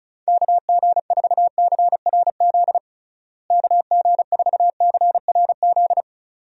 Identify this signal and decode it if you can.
Morse code, heard as a keyed tone, about 36 wpm, 700 Hz. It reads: KG4CRZ KG4CRZ
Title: USA Call Signs - 6 Characters - Repeated 2x - 35wpm